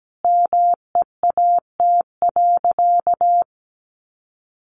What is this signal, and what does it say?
Morse code, heard as a keyed tone, about 17 wpm, 700 Hz. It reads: MEAT.
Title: Sentences from Top 1200 Words 17wpm